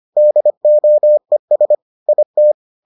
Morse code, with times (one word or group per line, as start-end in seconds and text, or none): 0.2-1.8 DOES
2.1-2.5 IT